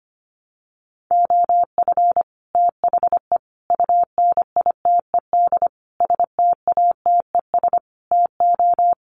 Morse code, heard as a keyed tone, about 25 wpm, 700 Hz. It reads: OF THE UNITED STATES TO
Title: Sets of 5 Words Encore 25wpm